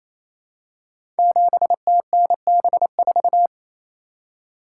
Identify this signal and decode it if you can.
Morse code, heard as a keyed tone, about 27 wpm, 700 Hz. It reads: 7TNB4
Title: Code Groups - 5 Letter:Numbers - 28wpm